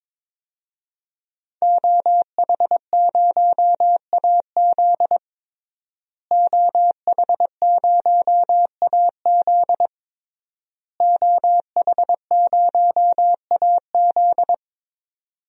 Morse code, heard as a keyed tone, about 22 wpm, 700 Hz. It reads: OH0AZ OH0AZ OH0AZ
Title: DXCC-100 Call Signs - 5 Characters - Repeated 3x - 22wpm